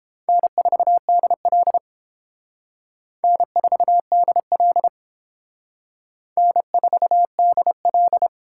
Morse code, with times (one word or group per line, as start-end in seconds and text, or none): 0.3-1.8 N4DL
3.2-4.9 N4DL
6.4-8.3 N4DL